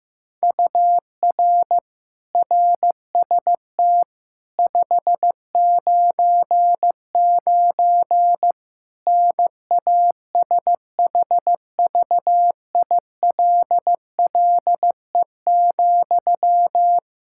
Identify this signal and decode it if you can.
Morse code, heard as a keyed tone, about 16 wpm, 700 Hz. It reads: UR RST 599 NASHVILLE,
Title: Practice QSO - Exchange 1 - Beginning - 15wpm